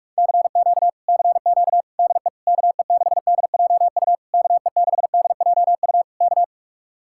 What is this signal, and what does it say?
Morse code, heard as a keyed tone, about 44 wpm, 700 Hz. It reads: CQ CQ DE KE6DJU KE6DJU K